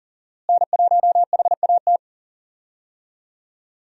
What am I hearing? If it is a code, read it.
Morse code, heard as a keyed tone, about 40 wpm, 700 Hz. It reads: N1HAT
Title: USA Call Signs - 5 Characters - Repeated 2x - 40wpm